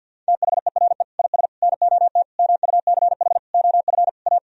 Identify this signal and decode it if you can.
Morse code, heard as a keyed tone, about 50 wpm, 700 Hz. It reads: THERE IS NOT MUCH OF A
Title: Sentences from Top 1000 Words - Review Part 4 - 50wpm